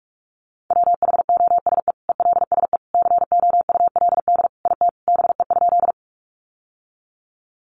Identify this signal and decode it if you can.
Morse code, heard as a keyed tone, about 44 wpm, 700 Hz. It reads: WHOSE ELSE COULD IT BE?
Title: Sentences from Top 1200 Words 45wpm